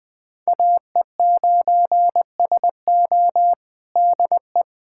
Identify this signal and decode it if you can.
Morse code, heard as a keyed tone, about 20 wpm, 700 Hz. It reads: AE9SO DE
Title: USA Callsign DE Callsign 20wpm